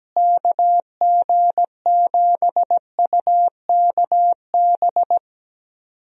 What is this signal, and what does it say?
Morse code, heard as a keyed tone, about 17 wpm, 700 Hz. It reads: KG7UKB
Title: USA Call Signs - 6 Characters - Repeated 3x - 17wpm